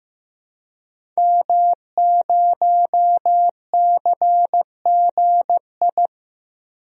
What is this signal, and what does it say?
Morse code, heard as a keyed tone, about 15 wpm, 700 Hz. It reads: M0CGI
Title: DXCC-100 Call Signs - 5 Characters - 15wpm